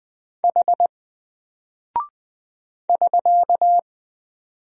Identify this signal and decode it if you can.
Morse code, heard as a keyed tone, about 20 wpm, 700 Hz. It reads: H <SK>
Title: Single Character (Letter:Number:Punctuation:Prosign) Alt 20wpm